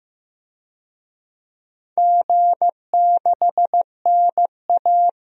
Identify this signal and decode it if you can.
Morse code, heard as a keyed tone, about 15 wpm, 700 Hz. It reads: G6NA
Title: DXCC-100 Call Signs - 4 Characters - Repeated 2x - 15wpm